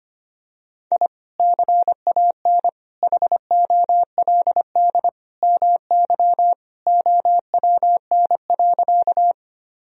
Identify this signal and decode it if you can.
Morse code, heard as a keyed tone, about 25 wpm, 700 Hz. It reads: I CAN HOLD MY OWN.
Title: Sentences from Top 1000 Words - Review Part 2 - 25wpm